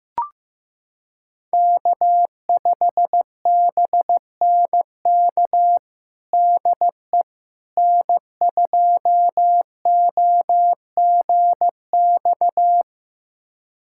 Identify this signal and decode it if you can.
Morse code, heard as a keyed tone, about 15 wpm, 700 Hz. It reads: K5BNK DE N2OGX
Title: USA Callsign DE Callsign - 5 Characters - 15wpm